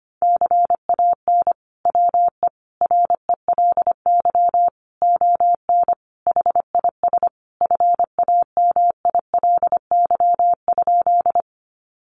Morse code, heard as a keyed tone, about 25 wpm, 700 Hz. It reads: CAN WE RELY ON HIS FAMILY?
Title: Sentences from Top 1200 Words 25wpm